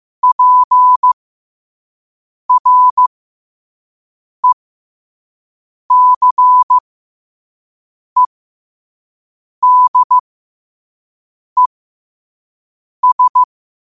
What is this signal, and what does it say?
Morse code, heard as a keyed tone, about 15 wpm, spaced as for 5 wpm, 1000 Hz. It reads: PRECEDES
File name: qotdfiles / 5wpm